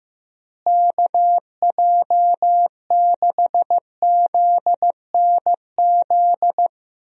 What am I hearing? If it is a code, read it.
Morse code, heard as a keyed tone, about 16 wpm, 700 Hz. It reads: KJ6ZNZ
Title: USA Call Signs - 6 Characters - Repeated 3x - 15wpm